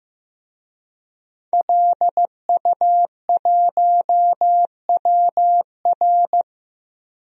Morse code, heard as a keyed tone, about 15 wpm, 700 Hz. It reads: LU1WR
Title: DXCC-100 Call Signs - 5 Characters - Repeated 3x - 15wpm